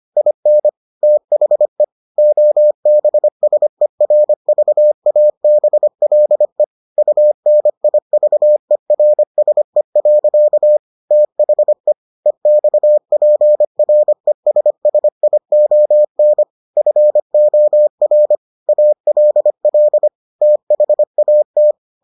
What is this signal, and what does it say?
Morse code, heard as a keyed tone, about 25 wpm, 600 Hz. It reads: IN THE OBSERVABLE UNIVERSE. THE EXPRESSION FOR ALL THAT